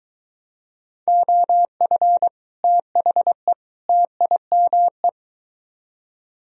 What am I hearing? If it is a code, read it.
Morse code, heard as a keyed tone, about 23 wpm, 700 Hz. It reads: OF THE TIME